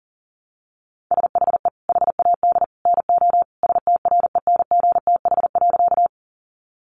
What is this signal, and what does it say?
Morse code, heard as a keyed tone, about 37 wpm, 700 Hz. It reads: SHE HAD NO STRENGTH.
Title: Sentences from Top 1300 Words 40wpm